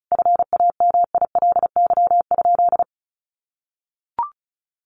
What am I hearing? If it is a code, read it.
Morse code, heard as a keyed tone, about 34 wpm, 700 Hz. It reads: FAMILY?
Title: Sentences from Top 1300 Words 35wpm